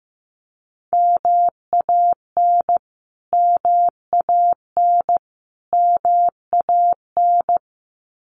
Morse code, heard as a keyed tone, about 15 wpm, 700 Hz. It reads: MAN MAN MAN
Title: Top 500 Words Repeated 15wpm